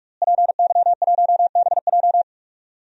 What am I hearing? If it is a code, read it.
Morse code, heard as a keyed tone, about 41 wpm, 700 Hz. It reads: PY1BJ
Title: DXCC-100 Call Signs - 5 Characters - Repeated 3x - 45wpm